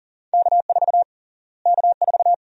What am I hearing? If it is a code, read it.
Morse code, heard as a keyed tone, about 40 wpm, 700 Hz. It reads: K4 K4
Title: USA Call Sign Prefixes - Repeated 2x - 40wpm